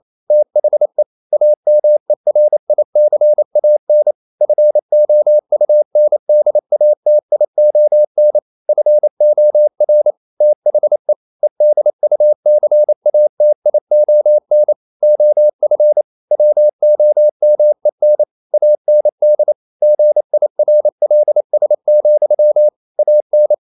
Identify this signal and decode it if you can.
Morse code, heard as a keyed tone, about 28 wpm, 600 Hz. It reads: THE AMERICAN FOUNDATION FOR THE EDUCATION OF WOMEN AND GIRLS, AN
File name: Never 0027wpm